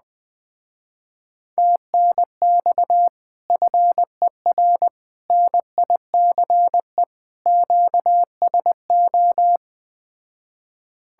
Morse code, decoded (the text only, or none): TNX FER NICE QSO